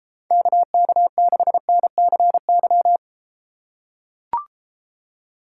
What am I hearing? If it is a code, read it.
Morse code, heard as a keyed tone, about 32 wpm, 700 Hz. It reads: KK6NCY